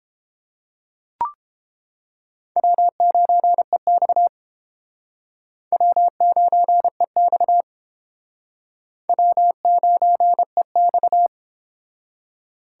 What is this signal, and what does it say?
Morse code, heard as a keyed tone, about 30 wpm, 700 Hz. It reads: W9EX W9EX W9EX